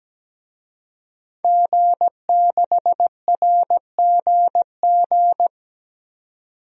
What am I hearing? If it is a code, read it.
Morse code, heard as a keyed tone, about 17 wpm, 700 Hz. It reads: G6RGG
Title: DXCC-100 Call Signs - 5 Characters - 17wpm